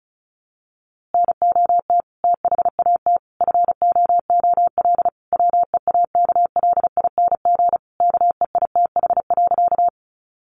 Morse code, heard as a keyed tone, about 34 wpm, 700 Hz. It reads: NOT THAT FOOL WEAKLING KEITH.
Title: A Sound Of Thunder 35wpm